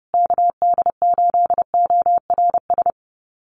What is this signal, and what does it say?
Morse code, heard as a keyed tone, about 30 wpm, 700 Hz. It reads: KD8ORS
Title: USA Call Signs Repeated 2x 30wpm